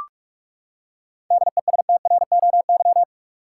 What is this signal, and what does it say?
Morse code, heard as a keyed tone, about 46 wpm, 700 Hz. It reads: DESTROY
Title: Seven Letter Words - 45wpm